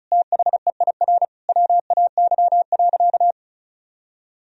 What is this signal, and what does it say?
Morse code, heard as a keyed tone, about 34 wpm, 700 Hz. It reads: THEIR WAY.
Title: Sentences from Top 1000 Words - Review Part 4 - 35wpm